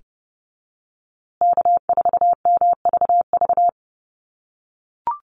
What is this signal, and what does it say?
Morse code, heard as a keyed tone, about 30 wpm, 700 Hz. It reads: K4MVV
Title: USA Call Signs Repeated 3x 30wpm